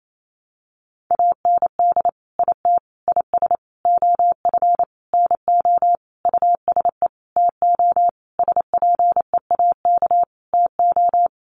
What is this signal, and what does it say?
Morse code, heard as a keyed tone, about 28 wpm, 700 Hz. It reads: AND IT IS OF NO USE TO SPEAK TO
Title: Sentences from Top 1100 Words 28wpm